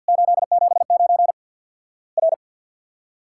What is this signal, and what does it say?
Morse code, heard as a keyed tone, about 50 wpm, 700 Hz. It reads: 879 R
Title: Worked All Europe DX Contest - QTC Exchange - 50wpm